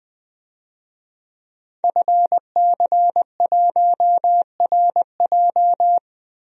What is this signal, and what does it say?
Morse code, heard as a keyed tone, about 20 wpm, 700 Hz. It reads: FC1RJ
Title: DXCC-100 Call Signs - 5 Characters - Repeated 2x - 20wpm